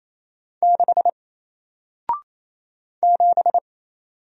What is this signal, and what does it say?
Morse code, heard as a keyed tone, about 27 wpm, 700 Hz. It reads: 6 7
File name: Single-Letter-Number-28wpm